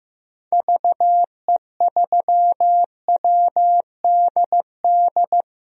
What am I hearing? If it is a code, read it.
Morse code, heard as a keyed tone, about 15 wpm, 700 Hz. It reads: VE3WDD
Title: DXCC-100 Call Signs - 6 Characters - 15wpm